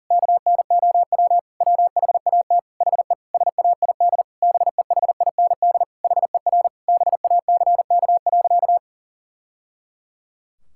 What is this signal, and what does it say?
Morse code, heard as a keyed tone, about 37 wpm, 700 Hz. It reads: KNOW WHAT HE SAID BEHIND HER BACK.
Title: Sentences from Top 1000 Words - Review Part 6 - 40wpm